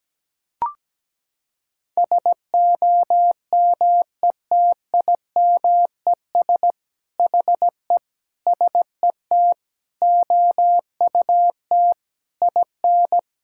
Morse code, heard as a keyed tone, about 17 wpm, 700 Hz. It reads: SOMETIMES HE SET OUT IN
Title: Sentences from Top 1000 Words - Review Part 5 - 17wpm